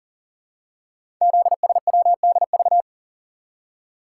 Morse code, heard as a keyed tone, about 40 wpm, 700 Hz. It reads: ZSWDV
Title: Code Groups - 5 Letters - 40wpm